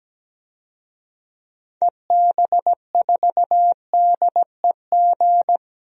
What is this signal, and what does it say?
Morse code, heard as a keyed tone, about 17 wpm, 700 Hz. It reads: EB4DEG
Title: DXCC-100 Call Signs - 6 Characters - Repeated 2x - 17wpm